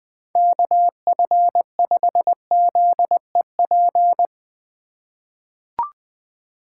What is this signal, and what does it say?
Morse code, heard as a keyed tone, about 20 wpm, 700 Hz. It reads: KF5ZEP
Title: USA Call Signs - 6 Characters - 2x Character Spacing - 20wpm